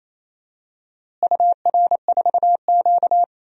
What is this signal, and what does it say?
Morse code, heard as a keyed tone, about 28 wpm, 700 Hz. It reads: UR4Q
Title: DXCC-100 Call Signs - 4 Characters - Repeated 2x - 28wpm